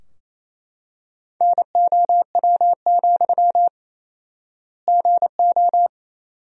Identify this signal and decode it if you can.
Morse code, heard as a keyed tone, about 28 wpm, 700 Hz. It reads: NOW, GO
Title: 4x Word-Spacing - Sentences from Top 200 Words - 28wpm